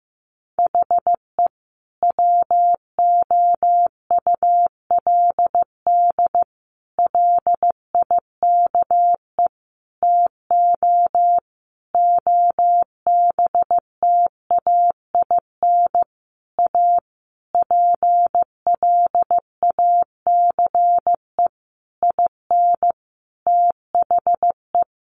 Morse code, heard as a keyed tone, about 15 wpm, 700 Hz. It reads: HE WOULD LIKE TO OBTAIN A PLACE IN THE
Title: Sentences from Top 1300 Words 15wpm